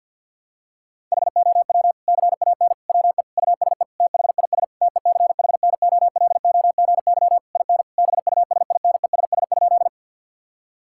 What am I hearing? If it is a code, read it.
Morse code, heard as a keyed tone, about 52 wpm, 700 Hz. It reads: HOW CAN WE USE THIS TECHNOLOGY IN BUSINESS?
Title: Sentences from Top 1000 Words - Review Part 5 - 50wpm